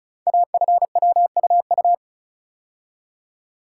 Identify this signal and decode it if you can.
Morse code, heard as a keyed tone, about 36 wpm, 690 Hz. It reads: AFWUU
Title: Code Groups - 5 Letters - 35wpm